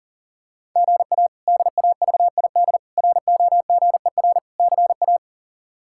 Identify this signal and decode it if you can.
Morse code, heard as a keyed tone, about 40 wpm, 700 Hz. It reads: GA DAVID ROGER CA
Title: K1USN SST QSO Practice - 40wpm